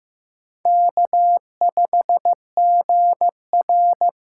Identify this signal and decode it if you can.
Morse code, heard as a keyed tone, about 15 wpm, 700 Hz. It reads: K5GR
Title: USA Call Signs - 3x Character Spacing - 15wpm